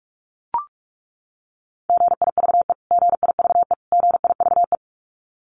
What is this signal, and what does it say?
Morse code, heard as a keyed tone, about 44 wpm, 700 Hz. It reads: GIVE GIVE GIVE
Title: Top 500 Words Repeated 45wpm